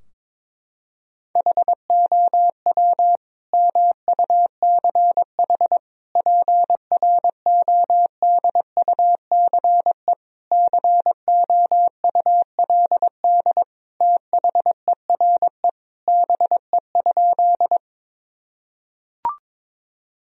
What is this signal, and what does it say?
Morse code, heard as a keyed tone, about 22 wpm, 700 Hz. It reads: HOW MUCH PRODUCE COULD THERE BE?
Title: Sentences from Top 1000 Words - Review Part 3 - 22wpm